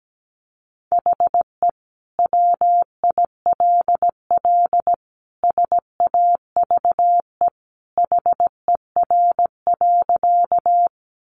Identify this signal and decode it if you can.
Morse code, heard as a keyed tone, about 17 wpm, 700 Hz. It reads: HE WILL SAVE HER.
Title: Sentences from Top 1100 Words 17wpm